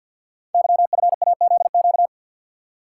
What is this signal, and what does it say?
Morse code, heard as a keyed tone, about 46 wpm, 700 Hz. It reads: YFAZQ
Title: Code Groups - 5 Letter:Numbers - 50wpm